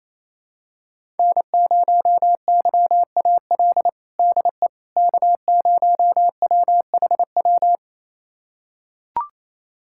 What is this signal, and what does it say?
Morse code, heard as a keyed tone, about 28 wpm, 700 Hz. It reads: N0YAL DE K0WHW
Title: USA Callsign DE Callsign - 5 Characters - 28wpm